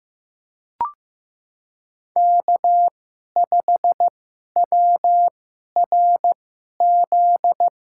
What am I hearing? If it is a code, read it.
Morse code, heard as a keyed tone, about 15 wpm, spaced as for 11 wpm, 700 Hz. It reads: K5WRZ
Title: USA Call Signs - 5 Characters - 2x Character Spacing - 15wpm